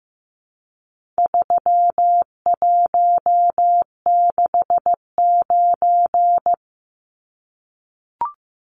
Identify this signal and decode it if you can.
Morse code, heard as a keyed tone, about 15 wpm, 700 Hz. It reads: 3169
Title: Numbers - 4-Digit 15wpm